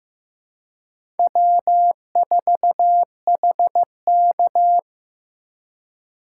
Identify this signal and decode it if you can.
Morse code, heard as a keyed tone, about 15 wpm, 700 Hz. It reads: W4HK
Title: USA Call Signs - 4 Characters - 2x Character Spacing - 15wpm